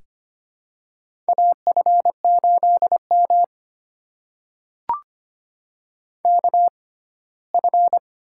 Morse code, heard as a keyed tone, about 25 wpm, spaced as for 17 wpm, 700 Hz. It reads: AF8M KF
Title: USA Call Signs - 6x Character Spacing - 25wpm